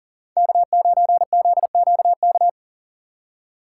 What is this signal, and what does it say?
Morse code, heard as a keyed tone, about 49 wpm, 700 Hz. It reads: K9ZQK
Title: Speed Racing - USA 1x3 Call Signs 40wpm